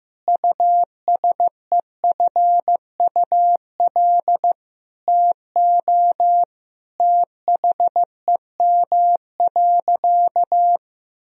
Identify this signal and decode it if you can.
Morse code, heard as a keyed tone, about 15 wpm, 700 Hz. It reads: USEFUL TO THEM.
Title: Sentences from Top 1000 Words - Review Part 4 - 15wpm